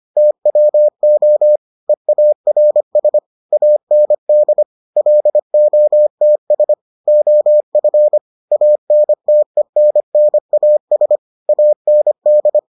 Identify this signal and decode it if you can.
Morse code, heard as a keyed tone, about 25 wpm, 600 Hz. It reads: TWO EARS AND LOTS OF ANTENNAS AND